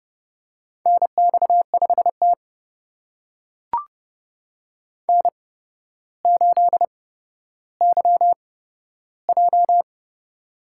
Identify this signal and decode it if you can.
Morse code, heard as a keyed tone, about 31 wpm, spaced as for 13 wpm, 700 Hz. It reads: NX5T N8YJ
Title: USA Call Signs - 4 Characters - 8x Character Spacing - 30wpm